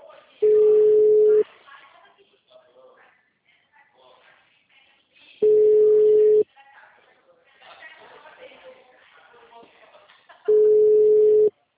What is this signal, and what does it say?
Ringback tone (Europe, 425 Hz), with speech.